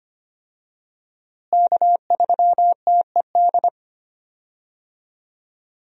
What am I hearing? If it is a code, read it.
Morse code, heard as a keyed tone, about 25 wpm, 700 Hz. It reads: K3TED
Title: USA Call Signs - 5 Characters - Repeated 3x - 25wpm